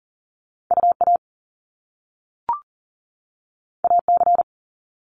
Morse code, heard as a keyed tone, about 39 wpm, 700 Hz. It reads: UA AC
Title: Top Two Letter Combinations in Words 40wpm